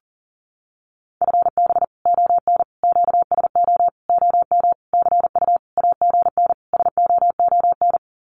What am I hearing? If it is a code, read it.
Morse code, heard as a keyed tone, about 40 wpm, 700 Hz. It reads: FB ON QSO OM CU AGN SOON
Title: Common QSO Phrases 40wpm